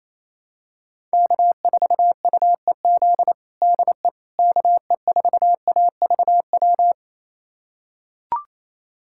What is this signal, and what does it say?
Morse code, heard as a keyed tone, about 28 wpm, 700 Hz. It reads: K4UEZ DE KE4AVW
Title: USA Callsign DE Callsign 28wpm